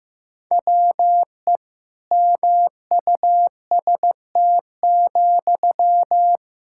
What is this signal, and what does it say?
Morse code, heard as a keyed tone, about 15 wpm, 700 Hz. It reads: WE MUST,
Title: Sentences from Top 1600 Words 15wpm